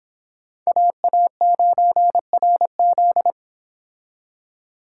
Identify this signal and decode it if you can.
Morse code, heard as a keyed tone, about 25 wpm, 700 Hz. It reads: AA9RZ